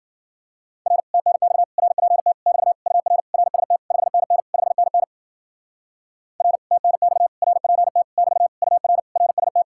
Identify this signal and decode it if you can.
Morse code, heard as a keyed tone, about 52 wpm, 700 Hz. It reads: R TNX RPT <BT> UR RST 5NN 5NN R TNX RPT <BT> UR RST